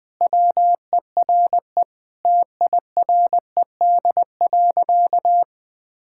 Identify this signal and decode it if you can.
Morse code, heard as a keyed tone, about 20 wpm, 700 Hz. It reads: WERE TIRED.
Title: Sentences with 5-Word Phrases - Spoken Beginning - 20wpm